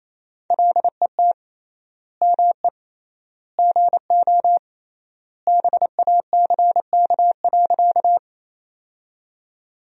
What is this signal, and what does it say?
Morse code, heard as a keyed tone, about 27 wpm, 700 Hz. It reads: LET ME GO BACK.
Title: 3x Word-Spacing - Sentences from Top 300 Words - 28wpm